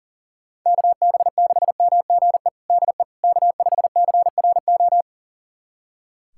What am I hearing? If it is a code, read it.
Morse code, heard as a keyed tone, about 40 wpm, 700 Hz. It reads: KB6MGE DE K5CRO
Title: USA Callsign DE Callsign 40wpm